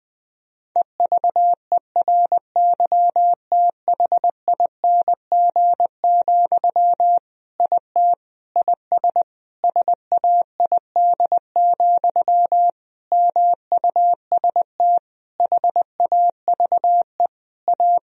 Morse code, heard as a keyed tone, about 21 wpm, 700 Hz. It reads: EVERYTHING, IT IS SAID, MUST HAVE A
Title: Sentences from Top 1000 Words - Review Part 1 - 20wpm